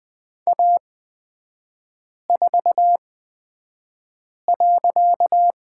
Morse code, heard as a keyed tone, about 20 wpm, spaced as for 5 wpm, 700 Hz. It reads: A4.
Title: 255 Lesson 41e - Period - Three Characters 20wpm